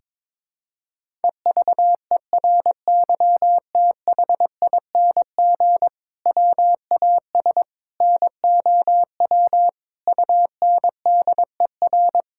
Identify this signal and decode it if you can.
Morse code, heard as a keyed tone, about 22 wpm, 700 Hz. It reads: EVERYTHING WAS NOW UNDER
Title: Sentences from Top 1000 Words - Review Part 5 - 22wpm